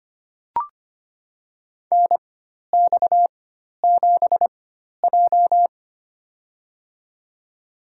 Morse code, heard as a keyed tone, about 25 wpm, spaced as for 12 wpm, 700 Hz. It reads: NX7J
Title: USA Call Signs - 4 Characters - 4x Character Spacing - 25wpm